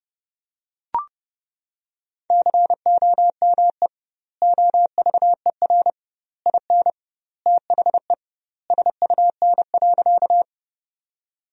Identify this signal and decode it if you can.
Morse code, heard as a keyed tone, about 30 wpm, 700 Hz. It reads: COME OVER IN THE SUN.
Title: 2x Word-Spacing - Sentences from Top 300 Words - 30wpm